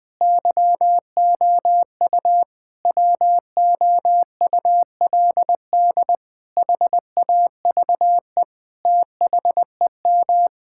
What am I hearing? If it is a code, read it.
Morse code, heard as a keyed tone, about 20 wpm, 700 Hz. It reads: YOU WOULD HAVE THEM
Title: Sentences with 8-Word Phrases - 20wpm